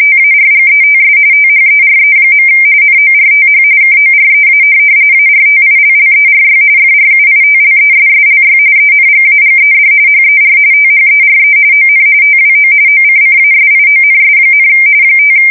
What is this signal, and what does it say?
RTTY, about 45.45 baud. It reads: TARGETS THERE. NORTH KOREA ARTILLERY FRUITLESSLY ENGAGED HER THERE ON TWO OCCASIONS, 5
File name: fsk170 / USS-Missouri